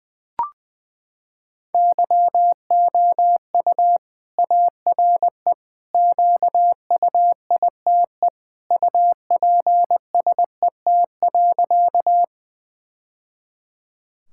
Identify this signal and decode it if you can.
Morse code, heard as a keyed tone, about 21 wpm, 700 Hz. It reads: YOU ARE QUITE UPSET.
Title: Sentences from Top 1000 Words - Review Part 5 - 20wpm